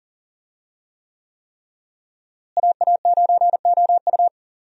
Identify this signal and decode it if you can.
Morse code, heard as a keyed tone, about 40 wpm, 700 Hz. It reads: AA9OU
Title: USA Call Signs - 5 Characters - Repeated 3x - 40wpm